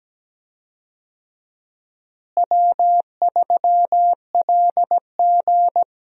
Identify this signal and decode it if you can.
Morse code, heard as a keyed tone, about 17 wpm, 700 Hz. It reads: W3LG
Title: USA Call Signs - 4 Characters - Repeated 3x - 17wpm